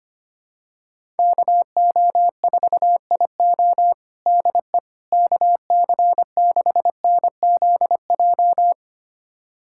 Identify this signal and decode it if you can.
Morse code, heard as a keyed tone, about 25 wpm, 700 Hz. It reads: KO4IO DE KC6NZJ
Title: USA Callsign DE Callsign 25wpm